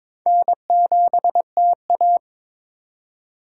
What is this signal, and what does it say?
Morse code, heard as a keyed tone, about 22 wpm, 700 Hz. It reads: N7TA
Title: USA Call Signs - 4 Characters - 4x Character Spacing - 22wpm